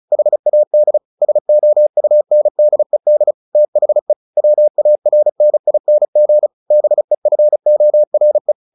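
Morse code, heard as a keyed tone, about 34 wpm, 600 Hz. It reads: HAD SOUNDED THE WARNING BEFORE
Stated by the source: Morse practice at 35 wpm from CW med Gnister, sending Ken Follett's novel Never in English.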